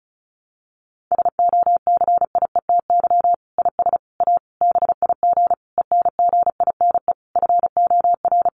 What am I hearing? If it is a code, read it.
Morse code, heard as a keyed tone, about 34 wpm, 700 Hz. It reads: SOCIETY IS A BIG ENGINE FOR
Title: Sentences from Top 1500 Words 35wpm